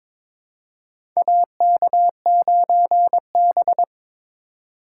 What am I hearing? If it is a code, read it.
Morse code, heard as a keyed tone, about 22 wpm, 700 Hz. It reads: AK9B
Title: USA Call Signs - 4 Characters - 4x Character Spacing - 22wpm